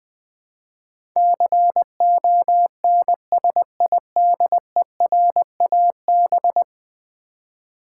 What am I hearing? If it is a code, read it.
Morse code, heard as a keyed tone, about 20 wpm, 700 Hz. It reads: CONSIDERAB
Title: Build A Word - 5 Letters At A Time - 12-Letter Words - 20wpm